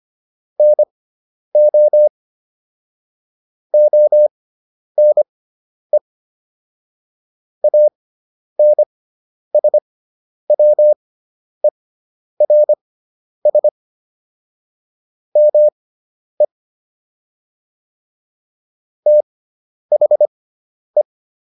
Morse code, heard as a keyed tone, about 25 wpm, spaced as for 10 wpm, 600 Hz. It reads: NO ONE ANSWERS ME THE